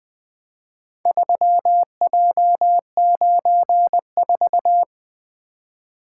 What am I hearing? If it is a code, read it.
Morse code, heard as a keyed tone, about 23 wpm, 700 Hz. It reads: 3J94
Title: Speed-Racing - Four Letter-Number Combinations - 20wpm